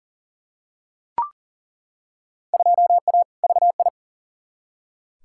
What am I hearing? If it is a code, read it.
Morse code, heard as a keyed tone, about 40 wpm, 700 Hz. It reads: 2A VI
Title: Field Day Exchanges - 40wpm